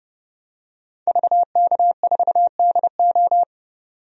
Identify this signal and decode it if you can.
Morse code, heard as a keyed tone, about 30 wpm, 700 Hz. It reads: VK4DO
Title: DXCC-100 Call Signs - 5 Characters - 30wpm